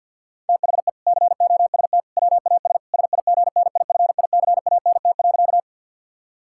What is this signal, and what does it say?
Morse code, heard as a keyed tone, about 50 wpm, 700 Hz. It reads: THE COST WAS SIGNIFICANT.
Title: Sentences from Top 1000 Words - Review Part 6 - 50wpm